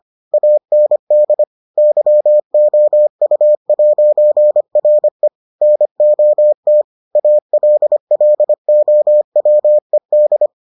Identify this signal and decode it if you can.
Morse code, heard as a keyed tone, about 25 wpm, 600 Hz. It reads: AND YOU'RE NOT ALLOWED